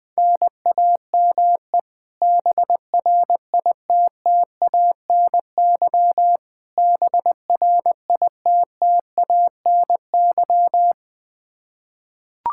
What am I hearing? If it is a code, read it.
Morse code, heard as a keyed tone, about 20 wpm, 700 Hz. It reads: NAME BRITTANY BRITTANY
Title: Practice QSO - Exchange 1 - Beginning - 20wpm